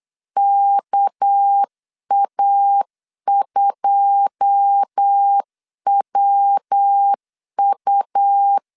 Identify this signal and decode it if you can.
Morse code, heard as a keyed tone, about 8 wpm, 800 Hz. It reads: KA2WU
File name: LEZIONI / 9 286 8wpm 3425